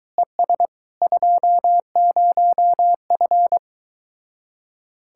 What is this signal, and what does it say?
Morse code, heard as a keyed tone, about 23 wpm, 700 Hz. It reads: ES 20F